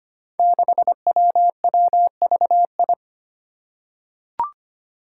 Morse code, heard as a keyed tone, about 25 wpm, 700 Hz. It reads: 6WWVI
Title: Five Challenging Letter-Number Combinations 25wpm